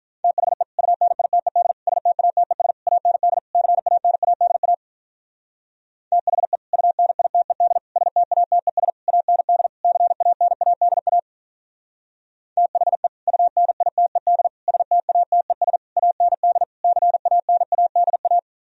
Morse code, heard as a keyed tone, about 43 wpm, 700 Hz. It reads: THE UNITED STATES AND CANADA THE UNITED STATES AND CANADA THE UNITED STATES AND CANADA
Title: Speed-Racing - Sets of 5 Words 35wpm